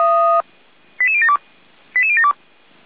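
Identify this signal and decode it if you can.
Five-tone selective call (ZVEI), heard as quick runs of five tones, tones of about 80 ms. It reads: 88072 88072
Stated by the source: Five-tone ZVEI selective call 88119 received by SDR.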